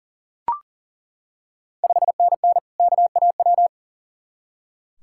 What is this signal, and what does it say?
Morse code, heard as a keyed tone, about 40 wpm, 700 Hz. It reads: 5NN KAW
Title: Ontario QSO Party Exchange 40wpm